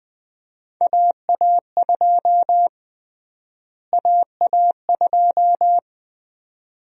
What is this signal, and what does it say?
Morse code, heard as a keyed tone, about 20 wpm, 700 Hz. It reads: AA2 AA2
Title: USA Call Sign Prefixes - Repeated 2x - 20wpm